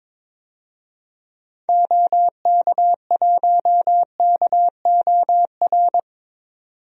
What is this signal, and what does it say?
Morse code, heard as a keyed tone, about 22 wpm, 700 Hz. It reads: OK1KOR
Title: DXCC-100 Call Signs - 6 Characters - 22wpm